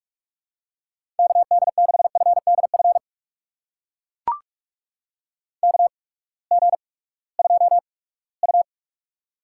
Morse code, heard as a keyed tone, about 44 wpm, spaced as for 22 wpm, 700 Hz. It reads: KD6FDF KG2U
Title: USA Call Signs - 8x Character Spacing - 45wpm